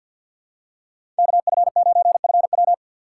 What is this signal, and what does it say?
Morse code, heard as a keyed tone, about 49 wpm, 700 Hz. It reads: KF9FW
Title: USA Call Signs - 5 Characters - 3x Character Spacing - 50wpm